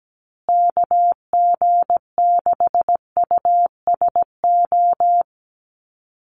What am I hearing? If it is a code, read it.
Morse code, heard as a keyed tone, about 17 wpm, 700 Hz. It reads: KG6USO
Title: USA Call Signs Repeated 3x 17wpm